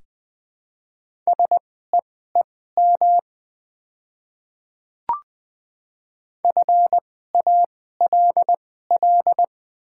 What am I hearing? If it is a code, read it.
Morse code, heard as a keyed tone, about 20 wpm, spaced as for 15 wpm, 700 Hz. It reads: SEEM FALL
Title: 2x Character Spacing - Top 500 Words - 20wpm